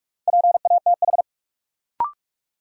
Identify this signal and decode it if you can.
Morse code, heard as a keyed tone, about 41 wpm, 700 Hz. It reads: PATH
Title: Sentences from Top 1400 Words - New Word Drill Alt - 45wpm